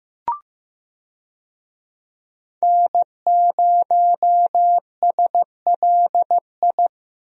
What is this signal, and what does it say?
Morse code, heard as a keyed tone, about 16 wpm, 700 Hz. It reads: N0SLI
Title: USA Call Signs - 5 Characters - Repeated 2x - 15wpm